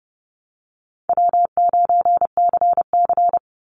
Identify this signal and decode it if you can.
Morse code, heard as a keyed tone, about 30 wpm, 700 Hz. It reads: W9CC
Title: Easy Call Signs Repeated 2x - 30wpm